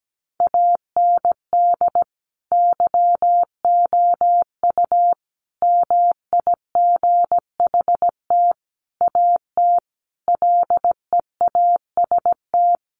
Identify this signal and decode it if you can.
Morse code, heard as a keyed tone, about 17 wpm, 700 Hz. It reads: AND YOU MIGHT AT LEAST
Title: Sentences from Top 1500 Words 17wpm